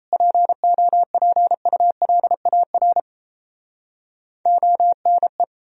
Morse code, heard as a keyed tone, about 38 wpm, 700 Hz. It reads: POPULAR ONE